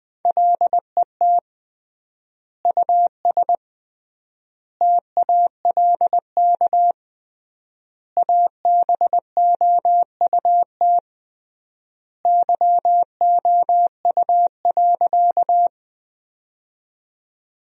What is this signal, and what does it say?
Morse code, heard as a keyed tone, about 20 wpm, 700 Hz. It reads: LET US TALK ABOUT YOU.
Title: 3x Word-Spacing - Sentences from Top 400 Words - 20wpm